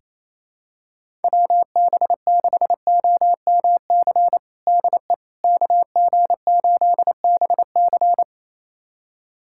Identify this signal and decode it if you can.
Morse code, heard as a keyed tone, about 28 wpm, 700 Hz. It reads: WB6OMC DE KG8BC
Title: USA Callsign DE Callsign 28wpm